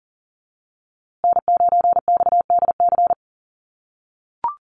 Morse code, heard as a keyed tone, about 40 wpm, 700 Hz. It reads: N9XDC
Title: USA Call Signs Repeated 2x 40wpm